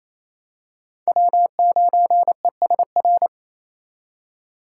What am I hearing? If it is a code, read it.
Morse code, heard as a keyed tone, about 27 wpm, 700 Hz. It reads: W9ESR
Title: USA Call Signs - 5 Characters - 4x Character Spacing - 28wpm